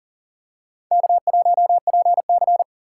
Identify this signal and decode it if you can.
Morse code, heard as a keyed tone, about 40 wpm, 700 Hz. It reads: K1PC
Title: USA Call Signs - 6x Character Spacing - 40wpm